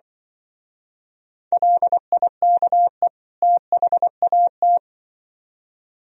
Morse code, heard as a keyed tone, about 24 wpm, 700 Hz. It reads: LIKE THAT
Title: Speed-Racing - Sets of 2 Words 20wpm